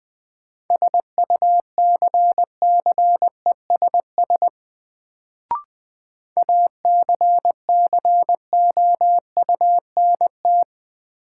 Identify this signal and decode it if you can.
Morse code, heard as a keyed tone, about 20 wpm, 700 Hz. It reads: SUCCESS ACCOUNT
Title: Sentences from Top 900 Words - New Word Drill - 20wpm